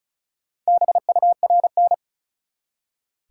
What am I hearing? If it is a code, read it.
Morse code, heard as a keyed tone, about 36 wpm, 700 Hz. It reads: BURN
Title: Four Letter Words - 35wpm